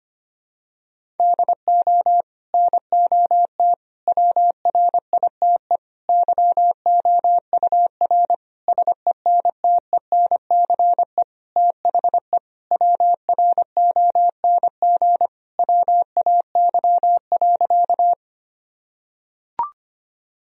Morse code, heard as a keyed tone, about 25 wpm, 700 Hz. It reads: DO NOT WRITE YOUR SENTENCE THE WRONG WAY.
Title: Sentences from Top 1000 Words - Review Part 1 - 25wpm